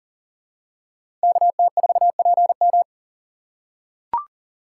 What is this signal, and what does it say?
Morse code, heard as a keyed tone, about 39 wpm, 700 Hz. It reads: KT4PM
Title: USA Call Signs - 5 Characters - 3x Character Spacing - 40wpm